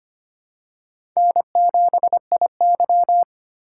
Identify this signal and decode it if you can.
Morse code, heard as a keyed tone, about 25 wpm, 700 Hz. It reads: N7IY
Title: USA Call Signs - 4 Characters - 3x Character Spacing - 25wpm